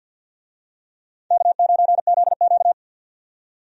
Morse code, heard as a keyed tone, about 50 wpm, 700 Hz. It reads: K9ZQ
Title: USA Call Signs - 4 Characters - 50wpm